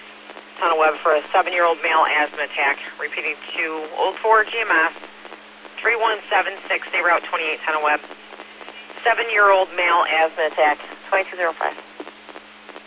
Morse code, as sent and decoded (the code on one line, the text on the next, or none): none